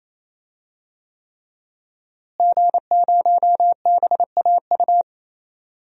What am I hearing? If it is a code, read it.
Morse code, heard as a keyed tone, about 28 wpm, 700 Hz. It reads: G0BAU
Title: DXCC-100 Call Signs - 5 Characters - Repeated 3x - 28wpm